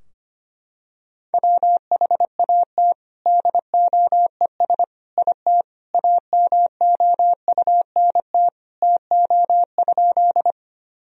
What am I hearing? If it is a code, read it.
Morse code, heard as a keyed tone, about 25 wpm, 700 Hz. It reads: WHAT DOES IT AMOUNT TO?
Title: Sentences from Top 1000 Words - Review Part 4 - 25wpm